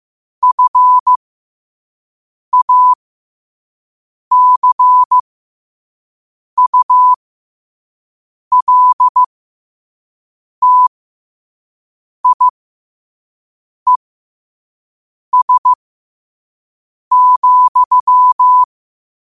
Morse code, heard as a keyed tone, about 15 wpm, spaced as for 5 wpm, 1000 Hz. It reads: FACULTIES,
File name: qotdfiles / 5wpm